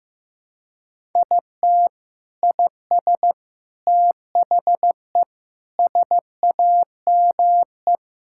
Morse code, heard as a keyed tone, about 15 wpm, 700 Hz. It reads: IT IS THE SAME
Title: Sentences from Top 1000 Words - Review Part 6 - 15wpm